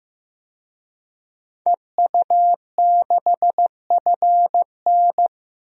Morse code, heard as a keyed tone, about 15 wpm, 700 Hz. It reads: EU6FN
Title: DXCC-100 Call Signs - 5 Characters - 15wpm